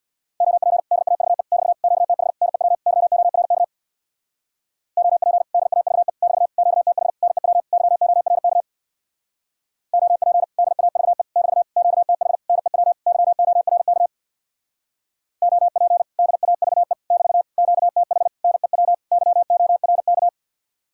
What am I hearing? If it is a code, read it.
Morse code, heard as a keyed tone, about 64 wpm, 700 Hz. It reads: OP DAVE <BT> QTH NEW YORK OP DAVE <BT> QTH NEW YORK OP DAVE <BT> QTH NEW YORK OP DAVE <BT> QTH NEW YORK